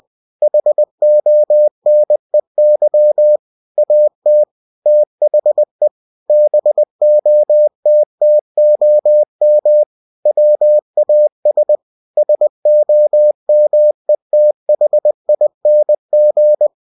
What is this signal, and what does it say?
Morse code, heard as a keyed tone, about 20 wpm, 600 Hz. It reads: HONEY AT THE BOTTOM WAS SOMETHING
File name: cw-books / winnie-the-pooh / winnie-20wpm-page-0040